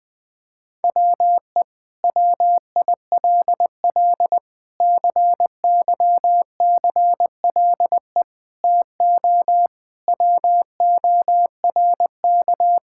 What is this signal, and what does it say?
Morse code, heard as a keyed tone, about 20 wpm, 700 Hz. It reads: WE WILL CYCLE TO WORK
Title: Sentences from Top 1000 Words - Review Part 6 - 20wpm